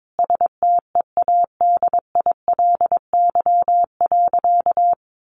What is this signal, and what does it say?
Morse code, heard as a keyed tone, about 22 wpm, 700 Hz. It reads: STEADILY.
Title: A Sound Of Thunder 22wpm